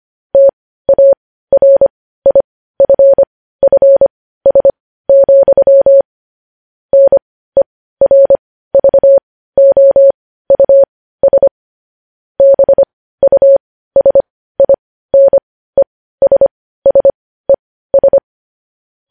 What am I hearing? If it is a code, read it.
Morse code, heard as a keyed tone, about 25 wpm, spaced as for 15 wpm, 550 Hz. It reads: TARIFFS, NERVOUS BUSINESSES
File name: podcast / News headlines-1x-25x15